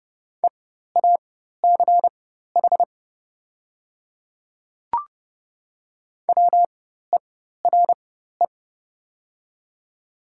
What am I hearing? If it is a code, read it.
Morse code, heard as a keyed tone, about 30 wpm, spaced as for 14 wpm, 700 Hz. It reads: EACH WERE
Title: 4x Character Spacing - Top 100 Words - 30wpm